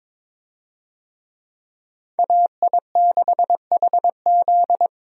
Morse code, heard as a keyed tone, about 22 wpm, 700 Hz. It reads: AI6HZ
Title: USA Call Signs - 5 Characters - Repeated 3x - 22wpm